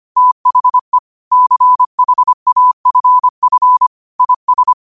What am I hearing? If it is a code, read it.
Morse code, heard as a keyed tone, about 25 wpm, 1000 Hz. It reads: THE CHAFF IS
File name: qotdfiles / 25wpm